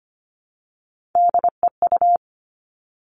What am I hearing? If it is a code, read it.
Morse code, heard as a keyed tone, about 25 wpm, 700 Hz. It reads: DEU
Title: Three Challenging Letter-Number Combinations 25wpm